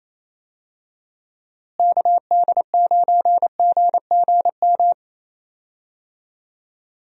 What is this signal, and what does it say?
Morse code, heard as a keyed tone, about 28 wpm, 700 Hz. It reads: KD9GGM
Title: USA Call Signs - 6 Characters - Repeated 3x - 28wpm